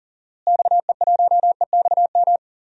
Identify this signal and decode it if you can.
Morse code, heard as a keyed tone, about 40 wpm, 700 Hz. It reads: XE1EXM
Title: DXCC-100 Call Signs - 6 Characters - Repeated 3x - 40wpm